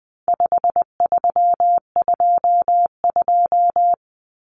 Morse code, heard as a keyed tone, about 20 wpm, 700 Hz. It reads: 5322
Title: Common Cut Numbers - 4-Digit 20wpm